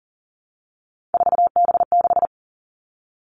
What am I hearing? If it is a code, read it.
Morse code, heard as a keyed tone, about 40 wpm, 700 Hz. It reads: 4B6
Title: Three Challenging Letter-Number Combinations 40wpm